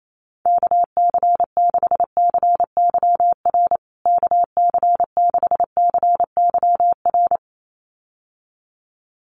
Morse code, heard as a keyed tone, about 28 wpm, 700 Hz. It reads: KC6CYR KC6CYR
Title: USA Call Signs Repeated 2x 28wpm